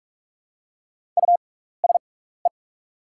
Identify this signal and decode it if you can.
Morse code, heard as a keyed tone, about 44 wpm, spaced as for 15 wpm, 700 Hz. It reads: USE
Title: 6x Character Spacing - Top 300 Words - 45wpm